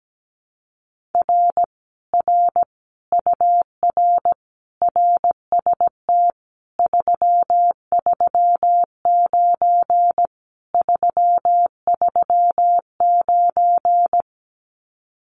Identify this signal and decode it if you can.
Morse code, heard as a keyed tone, about 17 wpm, 700 Hz. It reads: R R UR RST 339 339
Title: Common QSO Phrases Encore 17wpm